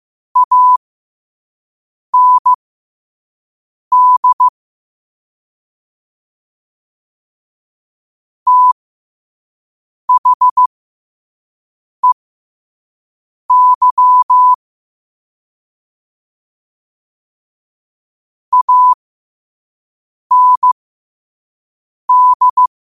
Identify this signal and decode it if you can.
Morse code, heard as a keyed tone, about 15 wpm, spaced as for 5 wpm, 1000 Hz. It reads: AND THEY AND
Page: New quotes every day in morse code at 5 Words per minute.